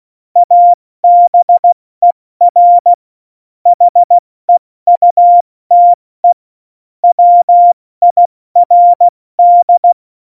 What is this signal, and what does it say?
Morse code, heard as a keyed tone, about 16 wpm, 700 Hz. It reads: ABER HEUTE WIRD
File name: Morsekurs / Witze / Witz 28. November 2017, 70 F80